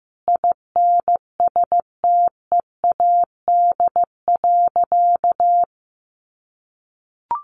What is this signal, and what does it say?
Morse code, heard as a keyed tone, about 15 wpm, 700 Hz. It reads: INSTEAD.
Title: Sentences from Top 1500 Words 15wpm